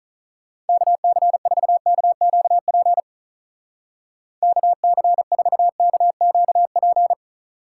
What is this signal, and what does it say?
Morse code, heard as a keyed tone, about 46 wpm, 700 Hz. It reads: KC4KQP KC4KQP